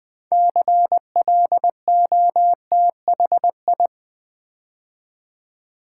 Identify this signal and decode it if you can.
Morse code, heard as a keyed tone, about 20 wpm, 700 Hz. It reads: CLOTHI
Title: Build A Word - 2 Letters At A Time - 8-Letter Words - 20wpm